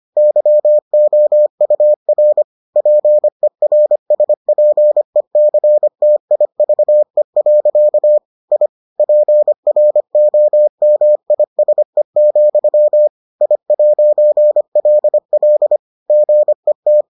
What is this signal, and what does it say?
Morse code, heard as a keyed tone, about 25 wpm, 600 Hz. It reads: YOUR PERSPECTIVE. I PROMISE, I'LL GET